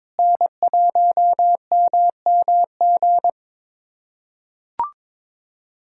Morse code, heard as a keyed tone, about 22 wpm, 700 Hz. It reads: N1MMG
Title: USA Callsign DE Callsign - 5 Characters - 22wpm